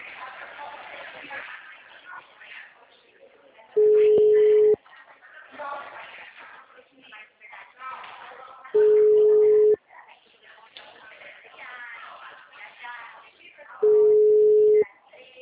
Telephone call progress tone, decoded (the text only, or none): ringback tone